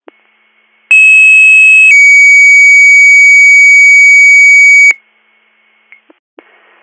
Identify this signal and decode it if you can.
Two-tone page, heard as a one-second tone, then a three-second tone, about 2600 then 2250 Hz. It is heard from 0.9 to 4.9 s.